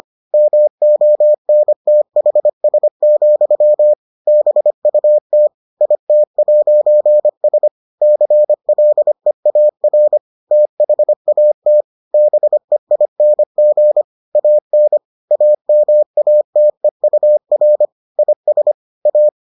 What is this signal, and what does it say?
Morse code, heard as a keyed tone, about 25 wpm, 600 Hz. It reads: MONTHS, BUT IT'S CLEAR THAT BEING AN AMATEUR IS A